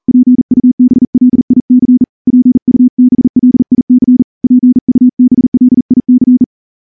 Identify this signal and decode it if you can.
Morse code, heard as a keyed tone, about 38 wpm, 250 Hz. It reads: PUBLIC PUBLIC PUBLIC
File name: morse-training-40WPM